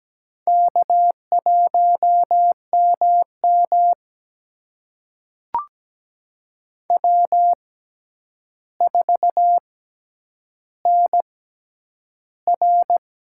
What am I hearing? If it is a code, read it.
Morse code, heard as a keyed tone, about 17 wpm, spaced as for 9 wpm, 700 Hz. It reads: K1MM W4NR
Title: USA Call Signs - 4 Characters - 6x Character Spacing - 17wpm